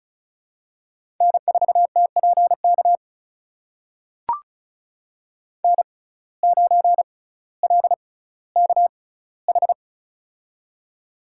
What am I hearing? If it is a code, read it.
Morse code, heard as a keyed tone, about 36 wpm, spaced as for 18 wpm, 700 Hz. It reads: N4TPK N9LKH
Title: USA Call Signs - 5 Characters - 6x Character Spacing - 35wpm